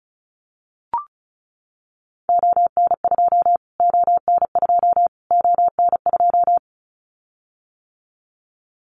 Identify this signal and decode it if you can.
Morse code, heard as a keyed tone, about 34 wpm, 700 Hz. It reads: ON2 ON2 ON2
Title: Callsign Prefixes for Top 25 DXCC - Repeated 3x - 35wpm